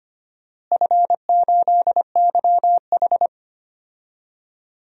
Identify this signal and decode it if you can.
Morse code, heard as a keyed tone, about 25 wpm, 700 Hz. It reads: F8YH
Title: DXCC-100 Call Signs - 4 Characters - Repeated 3x - 25wpm